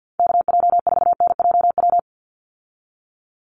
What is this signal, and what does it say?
Morse code, heard as a keyed tone, about 48 wpm, 700 Hz. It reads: KJ4NJW
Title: USA Call Signs 50wpm